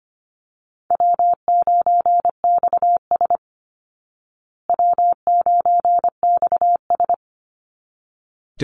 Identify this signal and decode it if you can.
Morse code, heard as a keyed tone, about 25 wpm, 700 Hz. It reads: W9XS W9XS
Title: Call Signs Repeated 3x - 25wpm